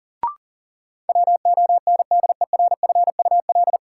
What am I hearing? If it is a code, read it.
Morse code, heard as a keyed tone, about 39 wpm, 700 Hz. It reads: WONDERFUL
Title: Sentences from Top 1100 Words - New Word Drill - 40wpm